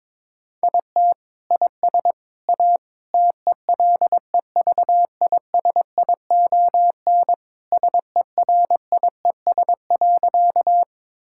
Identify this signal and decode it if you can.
Morse code, heard as a keyed tone, about 22 wpm, 700 Hz. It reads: IT IS A TELEVISION SERIES.
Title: Sentences from Top 1000 Words - Review Part 5 - 22wpm